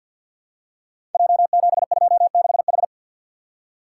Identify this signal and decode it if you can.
Morse code, heard as a keyed tone, about 52 wpm, 700 Hz. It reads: J726H
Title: Code Groups - 5 Letter:Numbers - 50wpm